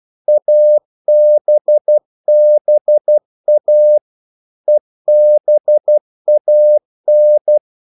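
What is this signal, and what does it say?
Morse code, heard as a keyed tone, about 12 wpm, 600 Hz. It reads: ABBA EBAN
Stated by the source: Morse practice sending English quotations.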